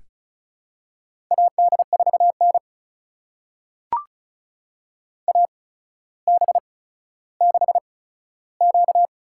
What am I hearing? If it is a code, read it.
Morse code, heard as a keyed tone, about 36 wpm, spaced as for 15 wpm, 700 Hz. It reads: AD4N AB6Q
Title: USA Call Signs - 4 Characters - 8x Character Spacing - 35wpm